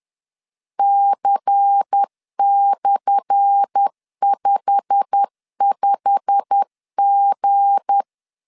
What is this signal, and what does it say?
Morse code, heard as a keyed tone, about 10 wpm, 800 Hz. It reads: C/55G
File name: LEZIONI / 10 SIGNS / 10A SLASH 10wpm 2808